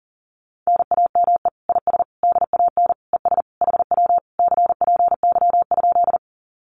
Morse code, heard as a keyed tone, about 42 wpm, 700 Hz. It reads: NAME IS DAN ES HW CPY?
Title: Common QSO Phrases Encore 40wpm